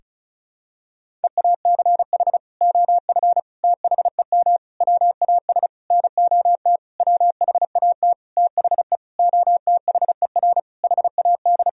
Morse code, heard as a keyed tone, about 36 wpm, 700 Hz. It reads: EACH OF THEM WAS NOT WHAT THE OTHER HAD
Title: Sentences from Top 1000 Words - Review Part 1 - 35wpm